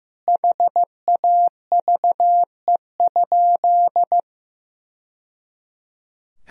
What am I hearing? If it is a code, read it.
Morse code, heard as a keyed tone, about 15 wpm, 700 Hz. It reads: HAVE?
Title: Sentences from Top 1000 Words - Review Part 1 - 15wpm